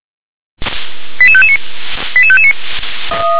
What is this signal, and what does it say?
Five-tone selective call (ZVEI), heard as quick runs of five tones, tones of about 70 ms. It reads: 88559 88559
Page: ZVEI